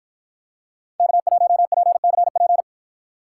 Morse code, heard as a keyed tone, about 53 wpm, 700 Hz. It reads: K1PCP